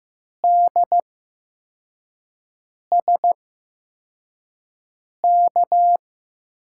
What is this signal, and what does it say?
Morse code, heard as a keyed tone, about 15 wpm, spaced as for 4 wpm, 700 Hz. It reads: DSK
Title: USA Call Signs - 6 Characters - 8x Character Spacing - 15wpm